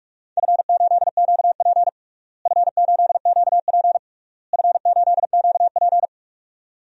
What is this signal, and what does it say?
Morse code, heard as a keyed tone, about 41 wpm, 700 Hz. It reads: F8QP F8QP F8QP
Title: DXCC-100 Call Signs - 4 Characters - Repeated 3x - 45wpm